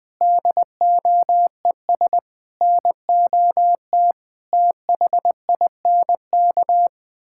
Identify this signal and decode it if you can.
Morse code, heard as a keyed tone, about 20 wpm, 700 Hz. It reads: DOES NOT THINK
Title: Sentences from Top 1600 Words 20wpm